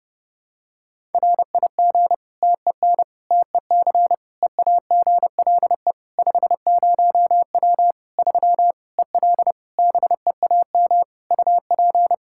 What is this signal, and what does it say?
Morse code, heard as a keyed tone, about 32 wpm, 700 Hz. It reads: RIG TEN TEC EAGLE 50W 3 EL BEAM UP
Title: Practice QSO - Exchange 2 - Info - 30wpm